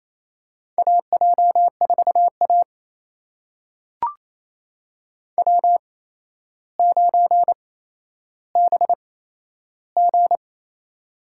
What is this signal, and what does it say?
Morse code, heard as a keyed tone, about 27 wpm, spaced as for 12 wpm, 700 Hz. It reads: AJ4A W9BG
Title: USA Call Signs - 4 Characters - 8x Character Spacing - 28wpm